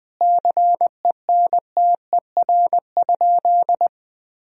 Morse code, heard as a keyed tone, about 20 wpm, 700 Hz. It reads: CENTER?
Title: Sentences from Top 1000 Words - Review Part 2 - 20wpm